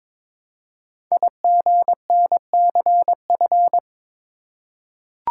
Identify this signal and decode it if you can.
Morse code, heard as a keyed tone, about 22 wpm, 700 Hz. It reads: IGNCF
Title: Code Groups - 5 Letters - 22wpm